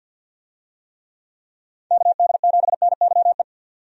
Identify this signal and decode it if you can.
Morse code, heard as a keyed tone, about 51 wpm, 700 Hz. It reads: KD7NYE
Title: USA Call Signs - 6 Characters - 50wpm